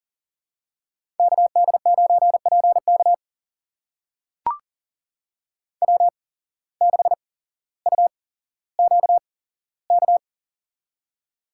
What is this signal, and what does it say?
Morse code, heard as a keyed tone, about 42 wpm, spaced as for 17 wpm, 700 Hz. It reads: KD9PK W6UQK
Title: USA Call Signs - 5 Characters - 8x Character Spacing - 40wpm